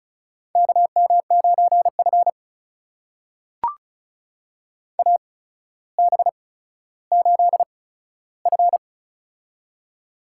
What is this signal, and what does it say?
Morse code, heard as a keyed tone, about 36 wpm, spaced as for 15 wpm, 700 Hz. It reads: KM9F AB8F
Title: USA Call Signs - 4 Characters - 8x Character Spacing - 35wpm